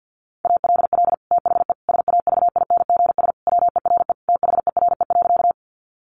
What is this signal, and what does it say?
Morse code, heard as a keyed tone, about 48 wpm, 700 Hz. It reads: ALL THE SAVINGS WERE THERE.
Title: Sentences from Top 1200 Words 50wpm